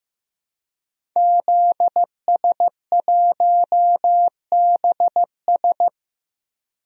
Morse code, heard as a keyed tone, about 15 wpm, 700 Hz. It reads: ZS1BS
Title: DXCC-100 Call Signs - 5 Characters - 15wpm